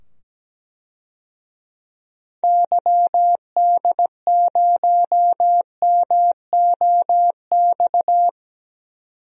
Morse code, heard as a keyed tone, about 17 wpm, 700 Hz. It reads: YD0MOX
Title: DXCC-100 Call Signs - 6 Characters - 17wpm